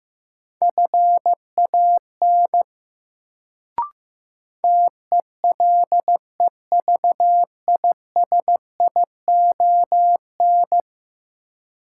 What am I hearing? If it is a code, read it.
Morse code, heard as a keyed tone, about 15 wpm, 700 Hz. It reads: FAN TELEVISION
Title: Sentences from Top 1000 Words - Review Part 4 - New Word Drill - 15wpm